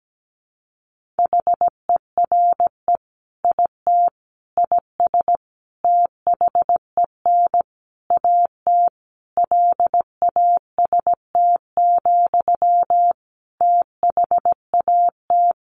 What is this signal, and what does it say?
Morse code, heard as a keyed tone, about 17 wpm, 700 Hz. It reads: HERE IT IS THEN AT LAST, THAT
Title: Sentences from Top 1500 Words 17wpm